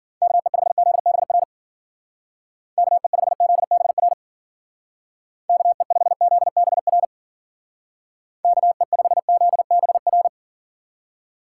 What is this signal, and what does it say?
Morse code, heard as a keyed tone, about 48 wpm, 700 Hz. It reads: KE5ZBR KE5ZBR KE5ZBR KE5ZBR
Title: Speed Racing - USA 2x3 Call Signs 40wpm